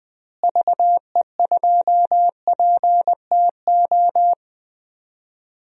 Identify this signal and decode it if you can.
Morse code, heard as a keyed tone, about 20 wpm, 700 Hz. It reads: VE2PTO
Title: DXCC-100 Call Signs - 6 Characters - Repeated 3x - 20wpm